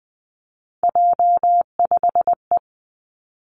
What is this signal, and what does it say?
Morse code, heard as a keyed tone, about 20 wpm, 700 Hz. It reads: J5E
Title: Three Challenging Letter-Number Combinations 20wpm